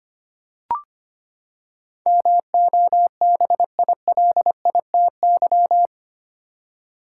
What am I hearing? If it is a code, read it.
Morse code, heard as a keyed tone, about 25 wpm, 700 Hz. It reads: MOBILITY
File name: Eight Letter Words - 25wpm